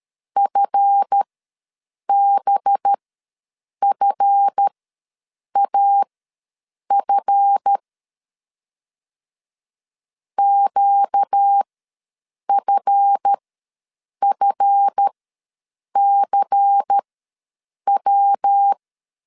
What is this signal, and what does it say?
Morse code, heard as a keyed tone, about 13 wpm, spaced as for 7 wpm, 800 Hz. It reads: FBFAF QFFCW
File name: LEZIONI / 7 GF9Z / 7 GF ws 12wpm 3140